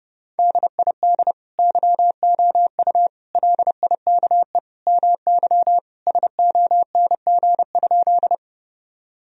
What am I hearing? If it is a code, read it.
Morse code, heard as a keyed tone, about 32 wpm, 700 Hz. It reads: DID YOU LIKE MY SONG?
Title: Sentences from Top 1000 Words - Review Part 2 - 30wpm